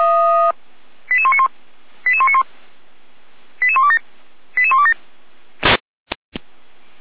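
Five-tone selective call (ZVEI), heard as quick runs of five tones, tones of about 70 ms. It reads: 88191 88191 88127 88127
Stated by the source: ZVEI five-tone selective call 88128 received by SDR, from the Feuerwehr Orlamünde audio archive.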